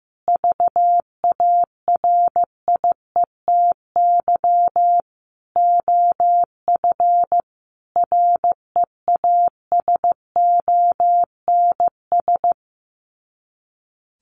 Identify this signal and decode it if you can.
Morse code, heard as a keyed tone, about 15 wpm, 700 Hz. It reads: VARIETY OF REASONS
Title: Sets of 5 Words Encore 15wpm